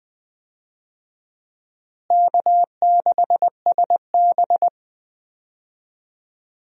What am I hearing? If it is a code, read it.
Morse code, heard as a keyed tone, about 20 wpm, 700 Hz. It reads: K6SB
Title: USA Call Signs - 4 Characters - Repeated 3x - 20wpm